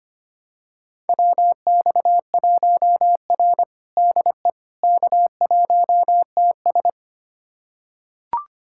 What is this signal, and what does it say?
Morse code, heard as a keyed tone, about 25 wpm, 700 Hz. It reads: WX1R DE K1TS
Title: USA Callsign DE Callsign - 4 Characters - 25wpm